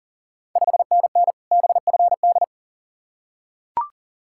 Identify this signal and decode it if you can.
Morse code, heard as a keyed tone, about 40 wpm, 690 Hz. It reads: 5NN BFD
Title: Ontario QSO Party Exchange 40wpm